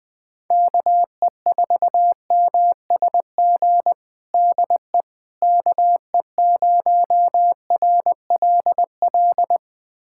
Morse code, heard as a keyed tone, about 20 wpm, 700 Hz. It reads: KE4MSG DE KE0RLL
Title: USA Callsign DE Callsign - 6 Characters - 20wpm